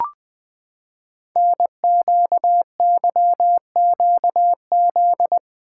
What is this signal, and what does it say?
Morse code, heard as a keyed tone, about 20 wpm, 700 Hz. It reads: NQYQZ
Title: Code Groups - 5 Letters - 20wpm